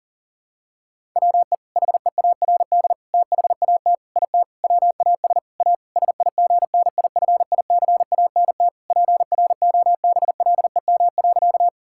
Morse code, heard as a keyed tone, about 37 wpm, 700 Hz. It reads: WE HEARD THAT IT WAS A SIGNIFICANT PROBLEM.
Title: Sentences from Top 1000 Words - Review Part 6 - 40wpm